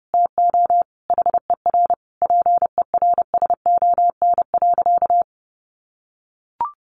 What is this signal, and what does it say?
Morse code, heard as a keyed tone, about 28 wpm, 700 Hz. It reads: TO HER PERSON.
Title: Sentences from Top 1400 Words 30wpm